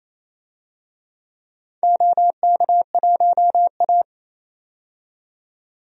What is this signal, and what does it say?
Morse code, heard as a keyed tone, about 28 wpm, 700 Hz. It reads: OK1A
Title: DXCC-100 Call Signs - 4 Characters - 28wpm